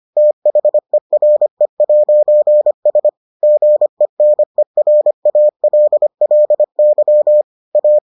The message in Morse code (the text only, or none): THERE'S GENERALLY A